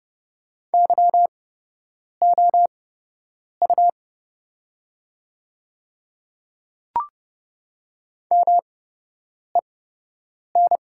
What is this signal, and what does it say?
Morse code, heard as a keyed tone, about 30 wpm, spaced as for 8 wpm, 700 Hz. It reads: YOU MEN
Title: 8x Character Spacing - Top 200 Words - 30wpm